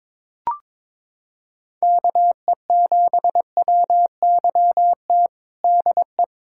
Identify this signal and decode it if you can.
Morse code, heard as a keyed tone, about 22 wpm, 700 Hz. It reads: KE7WYT DE
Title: USA Callsign DE Callsign 22wpm